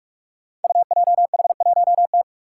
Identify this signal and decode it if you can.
Morse code, heard as a keyed tone, about 44 wpm, 700 Hz. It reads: UJH1T
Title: Five Challenging Letter-Number Combinations 45wpm